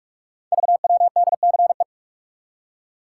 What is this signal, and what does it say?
Morse code, heard as a keyed tone, about 44 wpm, 700 Hz. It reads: VWDCE
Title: Code Groups - 5 Letters - 45wpm